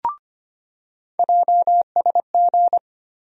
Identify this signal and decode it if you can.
Morse code, heard as a keyed tone, about 25 wpm, 700 Hz. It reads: JSG
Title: USA Call Sign Postfixes - 3 Letters 25wpm